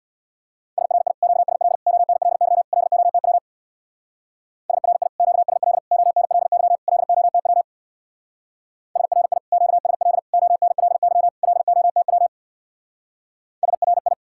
Morse code, heard as a keyed tone, about 66 wpm, 700 Hz. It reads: SRI QSL ONLY LOTW SRI QSL ONLY LOTW SRI QSL ONLY LOTW SRI